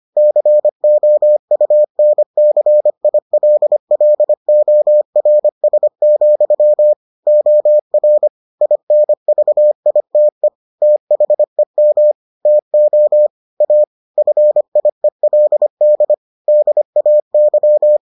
Morse code, heard as a keyed tone, about 25 wpm, 600 Hz. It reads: COUNCILLORS, OR INVITE THEM TO A FIELD DAY